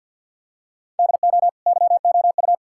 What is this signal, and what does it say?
Morse code, heard as a keyed tone, about 50 wpm, 700 Hz. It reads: DO YOU
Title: Sentences from Top 1000 Words - Review Part 6 - 50wpm